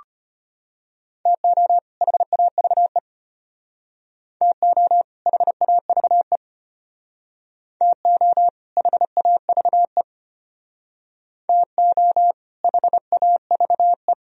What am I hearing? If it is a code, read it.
Morse code, heard as a keyed tone, about 33 wpm, 700 Hz. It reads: TO HAVE TO HAVE TO HAVE TO HAVE